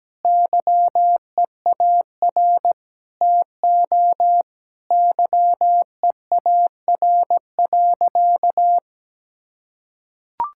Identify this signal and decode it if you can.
Morse code, heard as a keyed tone, about 17 wpm, 700 Hz. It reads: YEAR TO YEAR.
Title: Sentences from Top 1600 Words 17wpm